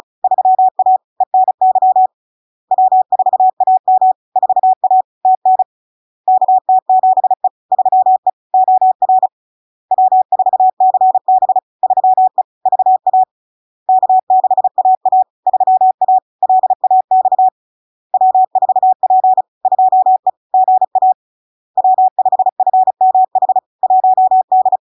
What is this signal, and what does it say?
Morse code, heard as a keyed tone, about 36 wpm, 750 Hz. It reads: 3A ENY W4AM 4A TN KT7E 3E OR W4CB 3E VA K6AA 3A LAX W4P 2E GA W5FMH 1D
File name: LARK-FD-2022-Log-in-CW-35wpm-750-Hz